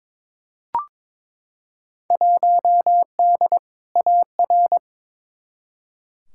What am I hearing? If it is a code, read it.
Morse code, heard as a keyed tone, about 22 wpm, 700 Hz. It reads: 1D AR
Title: Field Day Exchanges - 22wpm